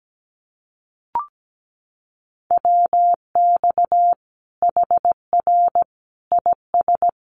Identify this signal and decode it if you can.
Morse code, heard as a keyed tone, about 17 wpm, 700 Hz. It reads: WX HR IS
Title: Common QSO Phrases 17wpm